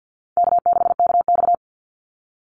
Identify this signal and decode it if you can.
Morse code, heard as a keyed tone, about 50 wpm, 700 Hz. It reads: K6KX
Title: Call Signs Repeated 2x - 50wpm